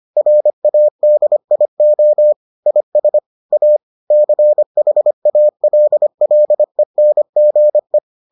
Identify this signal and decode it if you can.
Morse code, heard as a keyed tone, about 25 wpm, 600 Hz. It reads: RADIO IS A CHALLENGE